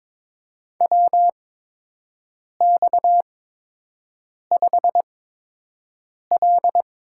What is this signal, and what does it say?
Morse code, heard as a keyed tone, about 21 wpm, spaced as for 6 wpm, 700 Hz. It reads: WX5L
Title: USA Call Signs - 8x Character Spacing - 22wpm